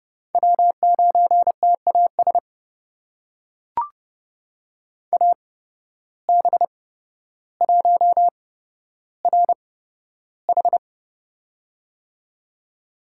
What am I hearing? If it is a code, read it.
Morse code, heard as a keyed tone, about 30 wpm, spaced as for 13 wpm, 700 Hz. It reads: W9TAS AB1RH
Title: USA Call Signs - 5 Characters - 8x Character Spacing - 30wpm